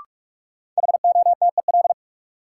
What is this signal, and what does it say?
Morse code, heard as a keyed tone, about 44 wpm, 700 Hz. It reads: HOTEL
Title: Sentences from Top 1600 Words - New Word Drill Alt - 45wpm